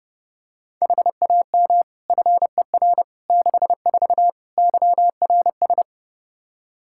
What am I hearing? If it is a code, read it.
Morse code, heard as a keyed tone, about 30 wpm, 700 Hz. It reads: HAM FER 64 YRS
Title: Practice QSO - Exchange 3 - More Info - 30wpm